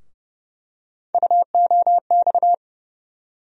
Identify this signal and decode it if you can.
Morse code, heard as a keyed tone, about 30 wpm, 700 Hz. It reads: UOX
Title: USA Call Sign Postfixes - 3 Letters 30wpm